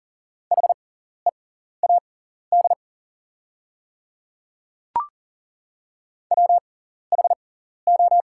Morse code, heard as a keyed tone, about 40 wpm, spaced as for 14 wpm, 690 Hz. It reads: HEAD WHO
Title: 6x Character Spacing - Top 300 Words - 40wpm